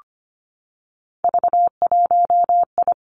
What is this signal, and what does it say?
Morse code, heard as a keyed tone, about 25 wpm, 700 Hz. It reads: V1I
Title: Three Challenging Letter-Number Combinations 25wpm